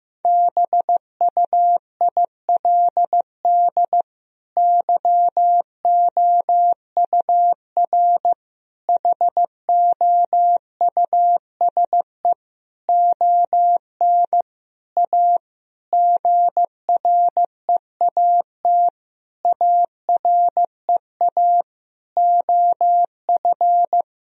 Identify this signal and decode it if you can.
Morse code, heard as a keyed tone, about 15 wpm, 700 Hz. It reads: BUILD YOUR HOUSE ON A GREAT AREA OF
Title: Sentences from Top 1000 Words - Review Part 2 - 15wpm